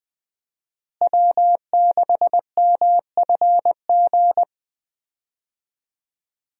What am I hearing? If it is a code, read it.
Morse code, heard as a keyed tone, about 20 wpm, 700 Hz. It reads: W6MFG
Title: USA Call Signs - 5 Characters - Repeated 3x - 20wpm